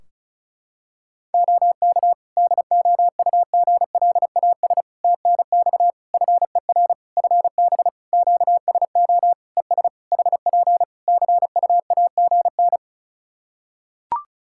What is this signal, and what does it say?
Morse code, heard as a keyed tone, about 36 wpm, 700 Hz. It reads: OK DOUGLAS TNX FER FB QSO ES HP CUAGN
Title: Practice QSO - Exchange 4 - Ending - 35wpm